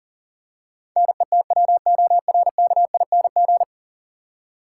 Morse code, heard as a keyed tone, about 40 wpm, 700 Hz. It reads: NETWORKING
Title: Practice QSO - Exchange 3 - More Info - 40wpm